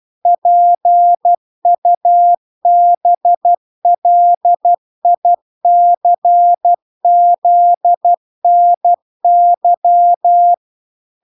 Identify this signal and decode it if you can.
Morse code, heard as a keyed tone, about 12 wpm, 700 Hz. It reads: PUBLICZNY